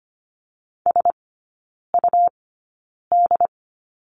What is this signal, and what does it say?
Morse code, heard as a keyed tone, about 25 wpm, spaced as for 9 wpm, 700 Hz. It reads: SUD
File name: Lesson 17e - Letter u - Three Characters 25wpm